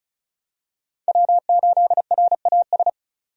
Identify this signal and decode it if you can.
Morse code, heard as a keyed tone, about 36 wpm, 700 Hz. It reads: W8RAS
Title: USA Call Signs - 5 Characters - 35wpm